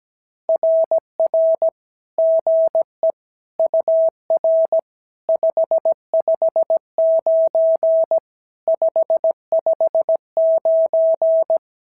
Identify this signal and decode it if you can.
Morse code, heard as a keyed tone, about 16 wpm, 650 Hz. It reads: RR GE UR 559 559
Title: SOTA QSO Practice 17wpm